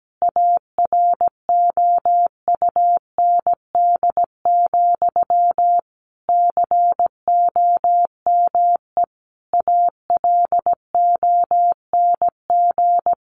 Morse code, heard as a keyed tone, about 17 wpm, 700 Hz. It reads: AROUND, COME ALONG
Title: Sentences from Top 1200 Words 17wpm